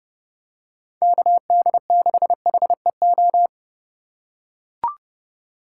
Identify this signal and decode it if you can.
Morse code, heard as a keyed tone, about 30 wpm, 700 Hz. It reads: KD6HEO
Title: USA Call Signs - 6 Characters - 3x Character Spacing - 30wpm